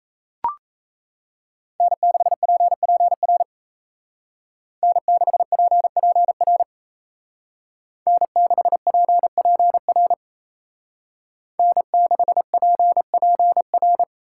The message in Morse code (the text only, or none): N6PPR N6PPR N6PPR N6PPR